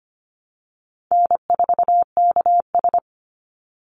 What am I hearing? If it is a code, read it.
Morse code, heard as a keyed tone, about 25 wpm, 700 Hz. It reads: N4KS
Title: Easy Call Signs Repeated 3x - 25wpm